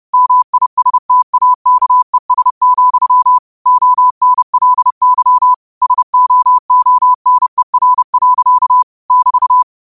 Morse code, heard as a keyed tone, about 30 wpm, 1000 Hz. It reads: MISTAKES, ONLY SOONER. <BT>
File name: qotdfiles / 30wpm